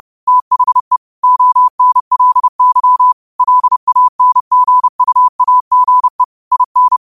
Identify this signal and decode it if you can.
Morse code, heard as a keyed tone, about 30 wpm, 1000 Hz. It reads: THE ONLY LANGUAGE IN
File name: qotdfiles / 30wpm